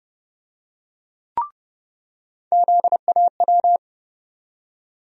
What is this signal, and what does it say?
Morse code, heard as a keyed tone, about 30 wpm, 700 Hz. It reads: ZAW
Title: USA Call Sign Postfixes - 3 Letters 30wpm